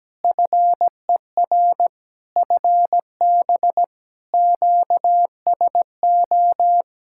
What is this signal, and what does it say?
Morse code, heard as a keyed tone, about 17 wpm, 700 Hz. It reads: FER FB QSO
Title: K1USN SST QSO Practice 17wpm